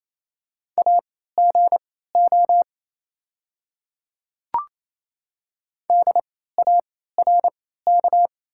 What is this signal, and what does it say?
Morse code, heard as a keyed tone, about 28 wpm, spaced as for 16 wpm, 690 Hz. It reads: AGO DARK
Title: 3x Character Spacing - Top 500 Words - 28wpm